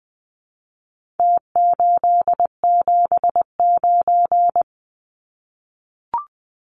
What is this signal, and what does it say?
Morse code, heard as a keyed tone, about 20 wpm, 700 Hz. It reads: T879
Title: Numbers - Contest Style - 4-Digit 20wpm